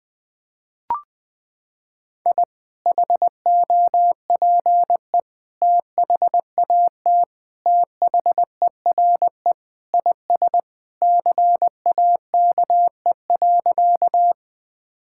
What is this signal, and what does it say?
Morse code, heard as a keyed tone, about 20 wpm, 700 Hz. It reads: I HOPE THAT THERE IS CAKE.
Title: Sentences with 2-Word Phrases - 20wpm